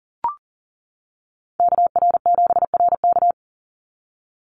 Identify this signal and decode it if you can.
Morse code, heard as a keyed tone, about 40 wpm, 700 Hz. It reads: KR7RK
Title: NA SOTA Activator Callsigns 40wpm